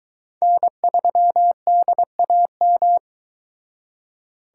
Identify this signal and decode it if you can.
Morse code, heard as a keyed tone, about 22 wpm, 700 Hz. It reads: N3DAM